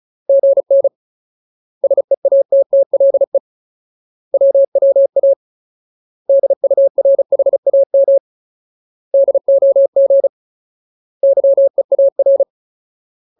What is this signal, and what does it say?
Morse code, heard as a keyed tone, about 34 wpm, 550 Hz. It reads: GN SEATTLE WWA DURHAM DOG YEAR
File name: Ultimate Word List / 35WPM Ultimate